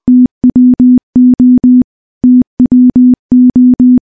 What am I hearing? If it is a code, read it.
Morse code, heard as a keyed tone, about 20 wpm, 250 Hz. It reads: TWO TWO
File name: morse-training-20WPM